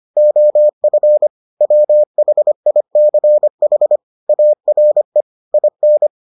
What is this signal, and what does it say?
Morse code, heard as a keyed tone, about 25 wpm, 600 Hz. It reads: OF WHICH ARE IN